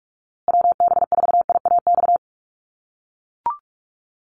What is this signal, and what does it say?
Morse code, heard as a keyed tone, about 44 wpm, 700 Hz. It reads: WB4IAX
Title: USA Call Signs Repeated 3x 45wpm